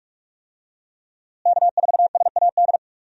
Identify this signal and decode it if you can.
Morse code, heard as a keyed tone, about 45 wpm, 700 Hz. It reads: K4SAD
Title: USA Call Signs - 5 Characters - Repeated 3x - 45wpm